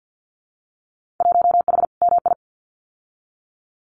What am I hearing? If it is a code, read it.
Morse code, heard as a keyed tone, about 50 wpm, 700 Hz. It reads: 1H MI
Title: Winter Field Day Exchanges 50wpm